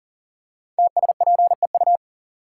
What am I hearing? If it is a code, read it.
Morse code, heard as a keyed tone, about 39 wpm, 700 Hz. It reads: TSPEU
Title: Code Groups - 5 Letter:Numbers - 40wpm